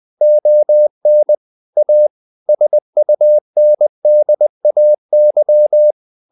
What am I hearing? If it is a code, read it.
Morse code, heard as a keyed tone, about 20 wpm, 600 Hz. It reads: ON A SUNDAY